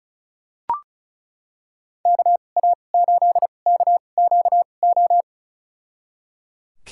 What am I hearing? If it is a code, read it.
Morse code, heard as a keyed tone, about 34 wpm, spaced as for 25 wpm, 700 Hz. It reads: KA8KQO
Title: USA Call Signs - 2x Character Spacing - 35wpm